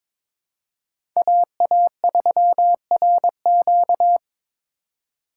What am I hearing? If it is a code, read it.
Morse code, heard as a keyed tone, about 22 wpm, 700 Hz. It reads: AA3RQ
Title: USA Call Signs - 2x Character Spacing - 22wpm